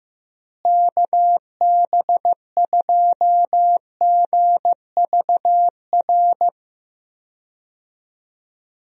Morse code, heard as a keyed tone, about 16 wpm, 700 Hz. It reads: KB2GVR
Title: USA Call Signs - 6 Characters - Repeated 3x - 15wpm